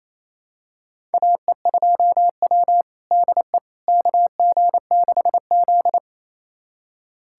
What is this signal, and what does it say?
Morse code, heard as a keyed tone, about 28 wpm, 700 Hz. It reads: AE2W DE KG6Z
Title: USA Callsign DE Callsign - 4 Characters - 28wpm